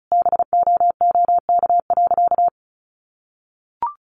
Morse code, heard as a keyed tone, about 34 wpm, 700 Hz. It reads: BOOK.
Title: Additional Easy Phrases 35wpm